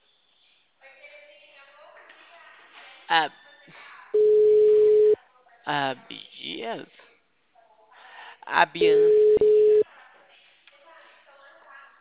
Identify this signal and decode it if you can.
Ringback tone (Europe, 425 Hz), with speech.
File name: out- -2008- - - .23778